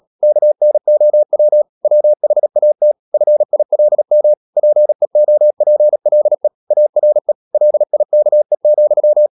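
Morse code, heard as a keyed tone, about 37 wpm, 600 Hz. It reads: KNOW WHAT FILM PEOPLE ARE LIKE,
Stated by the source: Morse code practice at 35–39 wpm sending English text of Ken Follett's novel Never, from CW med Gnister.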